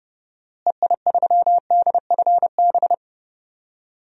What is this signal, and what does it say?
Morse code, heard as a keyed tone, about 31 wpm, 700 Hz. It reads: EI3DFB
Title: DXCC-100 Call Signs - 6 Characters - Repeated 3x - 30wpm